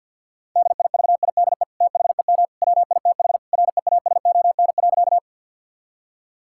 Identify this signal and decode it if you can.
Morse code, heard as a keyed tone, about 50 wpm, 700 Hz. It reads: DIVIDE THEM WITH REASON.
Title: Sentences from Top 1600 Words 50wpm